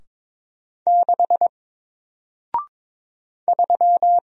Morse code, heard as a keyed tone, about 22 wpm, 700 Hz. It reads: 6 3
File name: Single-Letter-Number-22wpm